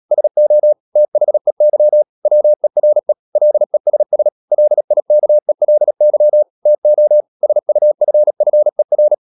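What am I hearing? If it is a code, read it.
Morse code, heard as a keyed tone, about 37 wpm, 600 Hz. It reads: SO THEY WERE LESS LIKELY TO SUFFER
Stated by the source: Morse code practice at 35–39 wpm sending English text of Ken Follett's novel Never, from CW med Gnister.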